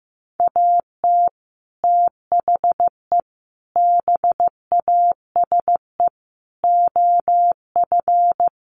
Morse code, heard as a keyed tone, about 15 wpm, 700 Hz. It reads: AT THE BASE OF
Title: Sets of 5 Words 15wpm